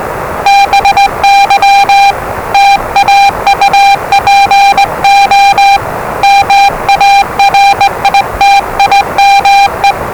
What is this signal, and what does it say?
Morse code, heard as a keyed tone, about 20 wpm, 800 Hz, with noise. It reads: BY TAUPO MARITIME
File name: WellingtonRadioZlwCloseDownOnHfAndMf / 01-Zlw 4277khz 417.5 qrt until 0925z